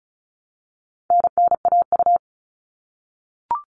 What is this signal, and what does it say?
Morse code, heard as a keyed tone, about 34 wpm, 700 Hz. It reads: NNAU
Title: All Cut Numbers - 4-Digit 35wpm